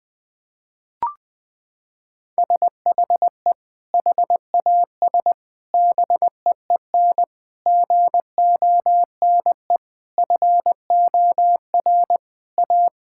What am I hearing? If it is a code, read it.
Morse code, heard as a keyed tone, about 20 wpm, 700 Hz. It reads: SHE HAS BEEN GONE FOR A
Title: Sentences with 3-Word Phrases - 20wpm